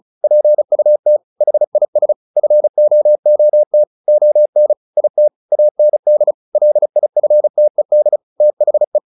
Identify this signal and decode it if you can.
Morse code, heard as a keyed tone, about 34 wpm, 600 Hz. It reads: PUT HIS FOOT ON IT AND LIFTED THE
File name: the old man and the sea 2 35wpm